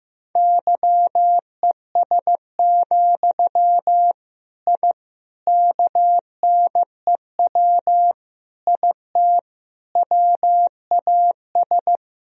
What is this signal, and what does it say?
Morse code, heard as a keyed tone, about 15 wpm, 700 Hz. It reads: YES, I KNEW IT WAS
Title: Sentences from Top 1000 Words - Review Part 2 - 15wpm